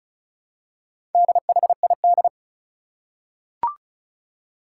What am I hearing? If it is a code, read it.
Morse code, heard as a keyed tone, about 36 wpm, 700 Hz. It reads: DHID
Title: Four Challenging Letter-Number Combinations 35wpm